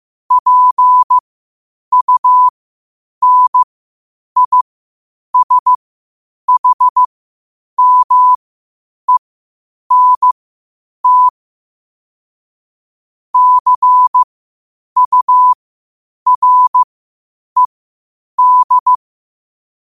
Morse code, heard as a keyed tone, about 15 wpm, spaced as for 8 wpm, 1000 Hz. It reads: PUNISHMENT CURED
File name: qotdfiles / 8wpm